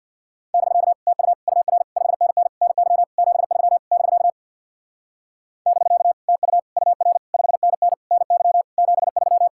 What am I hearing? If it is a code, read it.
Morse code, heard as a keyed tone, about 64 wpm, 700 Hz. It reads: <BK> TU UR 5NN NY 73 <BK> <BK> TU UR 5NN NY 73